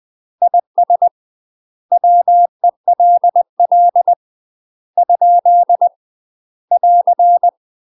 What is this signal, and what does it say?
Morse code, heard as a keyed tone, about 20 wpm, 700 Hz. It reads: IS WELL ? <AR>
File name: UK MASTER QSOs TRAINING / 20WPM / UK MASTER QSOs 0067